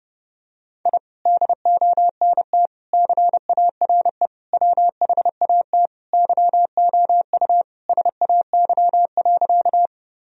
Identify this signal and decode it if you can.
Morse code, heard as a keyed tone, about 30 wpm, 700 Hz. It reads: I DONT CARE WHAT YOU SAY.
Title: Sentences from Top 1000 Words - Review Part 1 - 30wpm